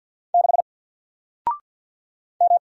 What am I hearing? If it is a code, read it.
Morse code, heard as a keyed tone, about 48 wpm, 700 Hz. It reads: 6 M
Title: Single Character (Letter:Number:Punctuation:Prosign) Alt 50wpm